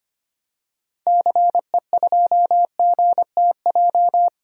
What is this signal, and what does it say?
Morse code, heard as a keyed tone, about 25 wpm, 700 Hz. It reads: CE2GTJ
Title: DXCC-100 Call Signs - 6 Characters - 25wpm